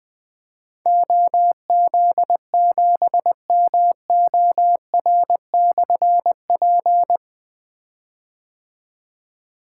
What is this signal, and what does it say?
Morse code, heard as a keyed tone, about 20 wpm, 700 Hz. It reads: OZ7MOR/P
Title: Difficult Call Signs 20wpm